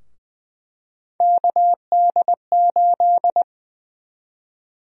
Morse code, heard as a keyed tone, about 20 wpm, 700 Hz. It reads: KD8
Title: USA Call Sign Prefixes - Repeated 2x - 20wpm